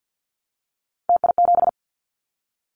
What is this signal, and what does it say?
Morse code, heard as a keyed tone, about 48 wpm, 700 Hz. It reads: TI7
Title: Callsign Prefixes for Top 100 DXCC - 50wpm